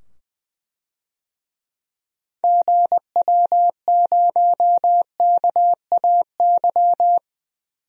Morse code, heard as a keyed tone, about 20 wpm, 700 Hz. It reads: GW0KAY
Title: DXCC-100 Call Signs - 6 Characters - 20wpm